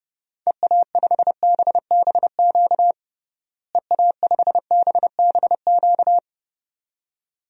Morse code, heard as a keyed tone, about 29 wpm, 700 Hz. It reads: EA5BBQ EA5BBQ
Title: DXCC-100 Call Signs - 6 Characters - Repeated 2x - 30wpm